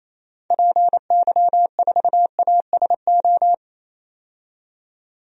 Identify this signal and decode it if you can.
Morse code, heard as a keyed tone, about 28 wpm, 700 Hz. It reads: PY4ASO
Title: DXCC-100 Call Signs - 6 Characters - 28wpm